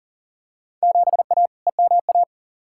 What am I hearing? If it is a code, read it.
Morse code, heard as a keyed tone, about 40 wpm, 700 Hz. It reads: 7A EMA
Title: Field Day Exchanges Encore - 40wpm